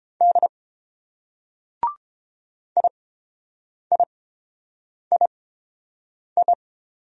Morse code, heard as a keyed tone, about 26 wpm, spaced as for 7 wpm, 700 Hz. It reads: D IIII
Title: Speed-Racing Single Letters 22wpm